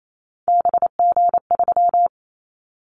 Morse code, heard as a keyed tone, about 28 wpm, 700 Hz. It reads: BG3
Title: Callsign Prefixes for Top 100 DXCC - 28wpm